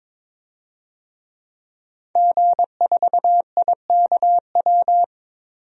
Morse code, heard as a keyed tone, about 22 wpm, 700 Hz. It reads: G4IKW
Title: DXCC-100 Call Signs - 5 Characters - 22wpm